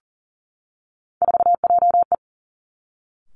Morse code, heard as a keyed tone, about 42 wpm, 690 Hz. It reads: 4JE
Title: Three Challenging Letter-Number Combinations 40wpm